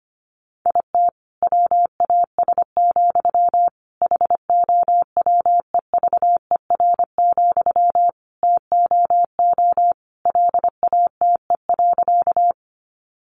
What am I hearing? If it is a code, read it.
Morse code, heard as a keyed tone, about 25 wpm, 700 Hz. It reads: IT WAS, HOWEVER, TOO LATE.
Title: Sentences from Top 600 Words 25wpm